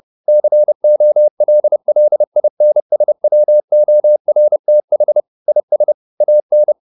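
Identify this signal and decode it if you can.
Morse code, heard as a keyed tone, about 30 wpm, 600 Hz. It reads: COLLINSWORTH IS AN
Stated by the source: Morse code at 30 wpm, the novel Never by Ken Follett in English, from CW med Gnister.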